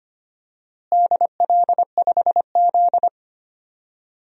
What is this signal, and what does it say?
Morse code, heard as a keyed tone, about 25 wpm, 700 Hz. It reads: DL5Z
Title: DXCC-100 Call Signs - 4 Characters - Repeated 2x - 25wpm